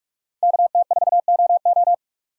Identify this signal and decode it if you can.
Morse code, heard as a keyed tone, about 46 wpm, 700 Hz. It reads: KT4OO
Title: USA Call Signs - 6x Character Spacing - 45wpm